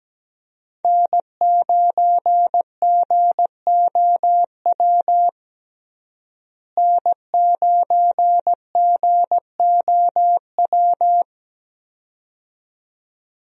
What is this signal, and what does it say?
Morse code, heard as a keyed tone, about 17 wpm, 700 Hz. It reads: N9GOW N9GOW
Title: USA Call Signs - 5 Characters - Repeated 2x - 17wpm